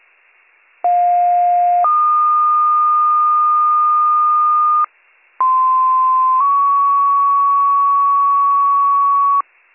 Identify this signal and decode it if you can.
Two-tone page, heard as a one-second tone, then a three-second tone, about 710 then 1220 Hz, about 1040 then 1120 Hz. It is heard from 0.8 to 9.4 s.